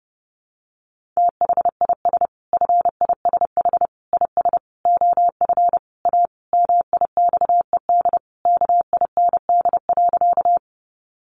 Morse code, heard as a keyed tone, about 30 wpm, 700 Hz. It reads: THIS FISH IS OF A MIXED KIND.
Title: Sentences from Top 1500 Words 30wpm